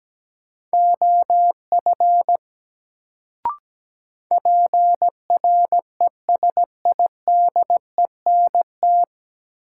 Morse code, heard as a keyed tone, about 17 wpm, 700 Hz. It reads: OF PRESIDENT
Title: Sets of 2 Words - New Word Drill - 17wpm